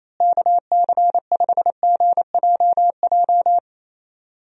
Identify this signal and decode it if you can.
Morse code, heard as a keyed tone, about 27 wpm, 700 Hz. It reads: KC5GJJ
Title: USA Call Signs - 6 Characters - Repeated 2x - 28wpm